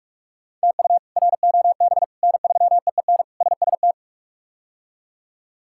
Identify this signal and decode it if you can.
Morse code, heard as a keyed tone, about 42 wpm, 700 Hz. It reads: TU ROB N3EEN SST
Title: K1USN SST QSO Practice - 45wpm